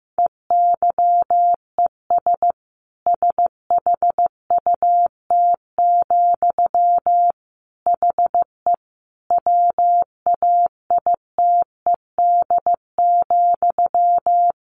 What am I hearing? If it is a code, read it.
Morse code, heard as a keyed tone, about 15 wpm, 700 Hz. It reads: EYES SHUT, HE WAITED,
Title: A Sound Of Thunder 15wpm